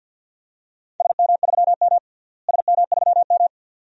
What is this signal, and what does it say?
Morse code, heard as a keyed tone, about 53 wpm, 700 Hz. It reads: SM3M SM3M
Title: DXCC-100 Call Signs - 4 Characters - Repeated 2x - 50wpm